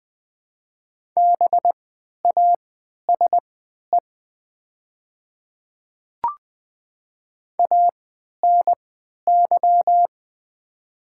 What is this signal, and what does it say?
Morse code, heard as a keyed tone, about 20 wpm, spaced as for 11 wpm, 700 Hz. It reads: BASE ANY
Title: 3x Character Spacing - Top 300 Words - 20wpm